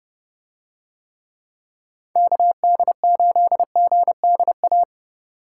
Morse code, heard as a keyed tone, about 24 wpm, 700 Hz. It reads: KD8GDA
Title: Speed Racing - USA Call Signs 20wpm